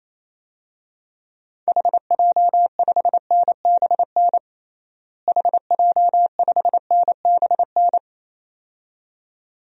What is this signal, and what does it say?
Morse code, heard as a keyed tone, about 28 wpm, 700 Hz. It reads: HJ5NBN HJ5NBN
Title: DXCC-100 Call Signs - 6 Characters - Repeated 2x - 28wpm